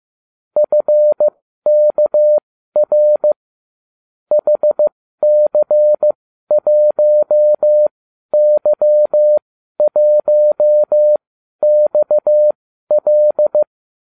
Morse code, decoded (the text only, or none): FKR HC1Y1XL